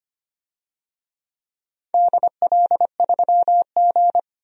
Morse code, heard as a keyed tone, about 25 wpm, 700 Hz. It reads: DL3G
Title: DXCC-100 Call Signs - 4 Characters - 25wpm